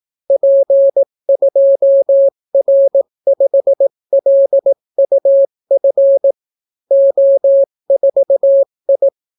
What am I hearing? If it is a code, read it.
Morse code, heard as a keyed tone, about 18 wpm, 550 Hz. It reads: P2R5LUF O4I